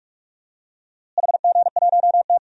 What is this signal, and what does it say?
Morse code, heard as a keyed tone, about 44 wpm, 700 Hz. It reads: HG1T
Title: DXCC-100 Call Signs - 4 Characters - Repeated 3x - 45wpm